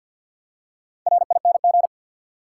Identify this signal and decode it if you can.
Morse code, heard as a keyed tone, about 51 wpm, 700 Hz. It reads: RING
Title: Four Letter Words - 50wpm